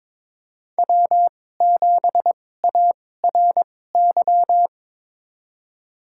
Morse code, heard as a keyed tone, about 22 wpm, spaced as for 16 wpm, 700 Hz. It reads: W7ARY
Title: USA Call Signs - 5 Characters - 2x Character Spacing - 22wpm